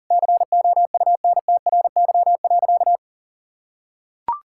Morse code, heard as a keyed tone, about 40 wpm, 700 Hz. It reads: COUNTRY.
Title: Sentences from Top 1600 Words 40wpm